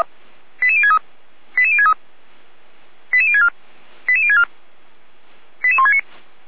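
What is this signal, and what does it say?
Five-tone selective call (ZVEI), heard as quick runs of five tones, tones of about 70 ms. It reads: 88063 88063 88064 88064 88169
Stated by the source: Five-tone ZVEI selective call received on an SDR.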